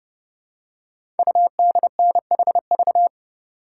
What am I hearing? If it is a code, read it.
Morse code, heard as a keyed tone, about 30 wpm, 700 Hz. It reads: UDNHV
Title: Code Groups - 5 Letters - 30wpm